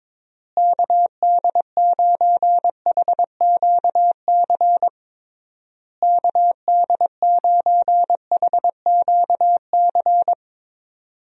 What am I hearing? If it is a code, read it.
Morse code, heard as a keyed tone, about 22 wpm, 700 Hz. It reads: KD9HQC KD9HQC
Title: USA Call Signs - 6 Characters - Repeated 3x - 22wpm